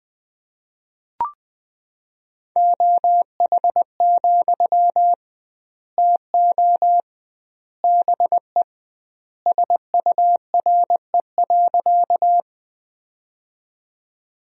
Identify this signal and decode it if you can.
Morse code, heard as a keyed tone, about 20 wpm, 700 Hz. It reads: OH, TO BE SURE.
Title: 2x Word-Spacing - Sentences from Top 500 Words - 20wpm